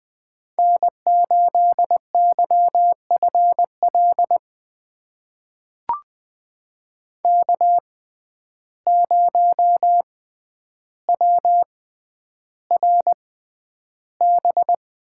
Morse code, heard as a keyed tone, about 21 wpm, spaced as for 10 wpm, 700 Hz. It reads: N8YFL K0WRB
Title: USA Call Signs - 5 Characters - 6x Character Spacing - 20wpm